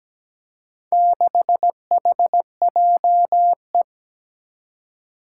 Morse code, heard as a keyed tone, about 17 wpm, 700 Hz. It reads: 6HJE
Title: Four Challenging Letter-Number Combinations 17wpm